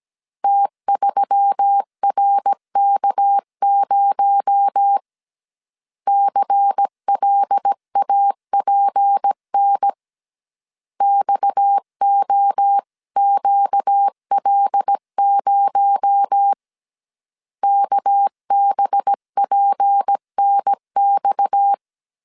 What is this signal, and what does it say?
Morse code, heard as a keyed tone, about 17 wpm, 800 Hz. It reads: T3RK0 CLAPN XOQL0 KBPNX
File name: CW / DET / 1 6 1500C S A 16wpm 2635